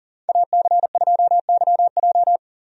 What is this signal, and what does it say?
Morse code, heard as a keyed tone, about 40 wpm, 700 Hz. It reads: AC2YJ
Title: USA Call Signs - 5 Characters - Repeated 2x - 40wpm